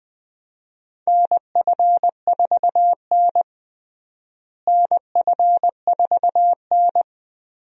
Morse code, heard as a keyed tone, about 20 wpm, 700 Hz. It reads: NF4N NF4N
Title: USA Call Signs - 4 Characters - Repeated 2x - 20wpm